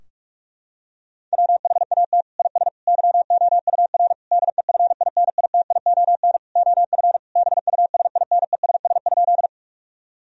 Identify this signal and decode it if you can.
Morse code, heard as a keyed tone, about 46 wpm, 700 Hz. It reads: WHAT IS YOUR DEFINITION OF BUSINESS?
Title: Sentences from Top 1000 Words - Review Part 4 - 45wpm